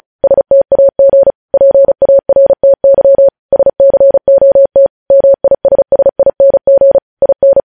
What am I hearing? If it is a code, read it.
Morse code, heard as a keyed tone, about 36 wpm, 550 Hz. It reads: STAG PARTY SCOT MISSING IN
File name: podcast / News headlines-1x-35x35